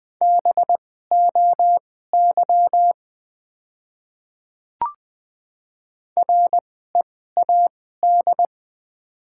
Morse code, heard as a keyed tone, about 21 wpm, spaced as for 15 wpm, 700 Hz. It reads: BOY READ
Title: 2x Character Spacing - Top 500 Words - 20wpm